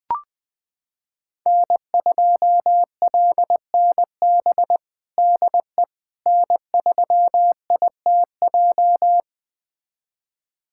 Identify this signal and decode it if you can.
Morse code, heard as a keyed tone, about 20 wpm, 700 Hz. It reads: N2LNB DE N3ITJ
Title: USA Callsign DE Callsign - 5 Characters - 20wpm